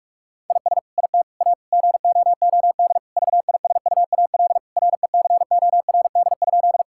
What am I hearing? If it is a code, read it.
Morse code, heard as a keyed tone, about 46 wpm, 700 Hz. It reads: IS IT A GOOD VISUAL RECORD?
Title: Sentences from Top 1000 Words - Review Part 6 - 45wpm